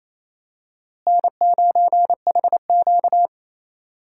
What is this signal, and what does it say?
Morse code, heard as a keyed tone, about 28 wpm, 700 Hz. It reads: N9HQ
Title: USA Call Signs - 4 Characters - Repeated 3x - 28wpm